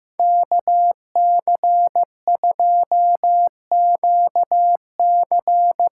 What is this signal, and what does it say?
Morse code, heard as a keyed tone, about 15 wpm, 700 Hz. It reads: KC2QC
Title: USA Call Signs - 6 Characters - 8x Character Spacing - 15wpm